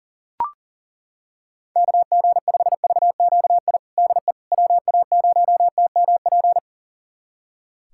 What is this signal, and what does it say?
Morse code, heard as a keyed tone, about 37 wpm, 700 Hz. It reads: KG5VQI DE WA0TMP
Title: USA Callsign DE Callsign 40wpm